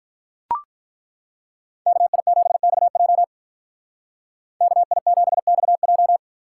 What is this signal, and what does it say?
Morse code, heard as a keyed tone, about 43 wpm, 700 Hz. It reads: KI7XJ KI7XJ
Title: Speed Racing - USA 2x2 Call Signs 35wpm